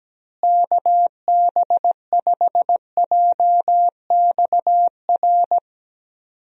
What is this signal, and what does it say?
Morse code, heard as a keyed tone, about 17 wpm, 700 Hz. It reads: KB5JXR
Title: USA Call Signs - 6 Characters - Repeated 3x - 17wpm